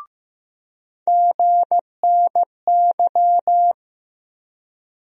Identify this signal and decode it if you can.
Morse code, heard as a keyed tone, about 15 wpm, 700 Hz. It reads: GNY
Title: USA Call Sign Postfixes - 3 Letters 15wpm